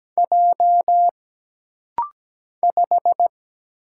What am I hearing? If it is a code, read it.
Morse code, heard as a keyed tone, about 17 wpm, 700 Hz. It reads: J 5
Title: Single Letter-Number - Rapid-Fire 17wpm